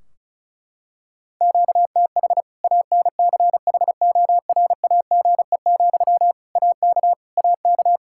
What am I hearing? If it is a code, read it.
Morse code, heard as a keyed tone, about 34 wpm, 700 Hz. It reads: QTH ANCHORAGE, AK AK
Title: Practice QSO - Exchange 1 - Beginning - 35wpm